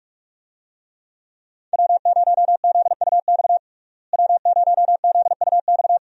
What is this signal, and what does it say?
Morse code, heard as a keyed tone, about 44 wpm, 700 Hz. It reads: W0ZUX W0ZUX
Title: USA Call Signs - 5 Characters - Repeated 2x - 45wpm